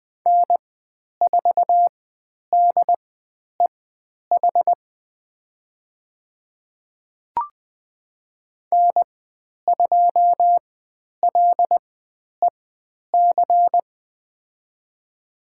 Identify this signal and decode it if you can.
Morse code, heard as a keyed tone, about 20 wpm, spaced as for 10 wpm, 700 Hz. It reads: N4DEH N2LEC
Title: 095 Lesson 16g - Letter c - Call Signs 20wpm